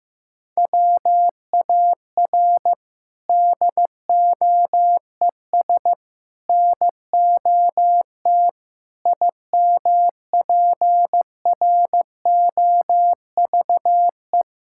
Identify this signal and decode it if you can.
Morse code, heard as a keyed tone, about 15 wpm, 700 Hz. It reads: WAR DOES NOT IMPROVE
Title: Sentences from Top 1000 Words - Review Part 5 - 15wpm